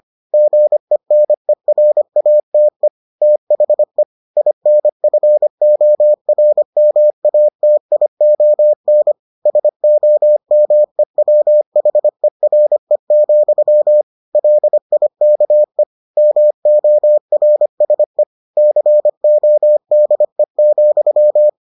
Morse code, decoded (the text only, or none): GENERATE THE INFORMATION SOMEWHERE, LIKE MORSE CODE,